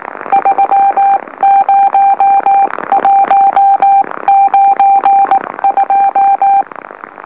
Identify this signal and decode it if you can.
Morse code, heard as a keyed tone, about 19 wpm, 790 Hz. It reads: 30192
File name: M04- -2345utc-LOLO